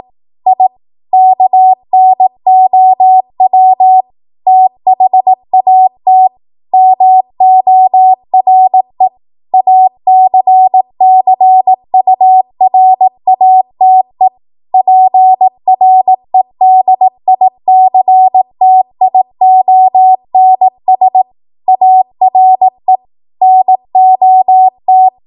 Morse code, heard as a keyed tone, about 18 wpm, 750 Hz. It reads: I KNOW THAT MORE ACCURATE PREDICTIONS ARE NOT
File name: Morse / 18WPM